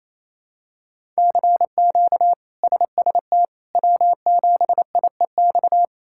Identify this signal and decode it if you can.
Morse code, heard as a keyed tone, about 28 wpm, 700 Hz. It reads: CQ SST W7IEX
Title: K1USN SST QSO Practice - 28wpm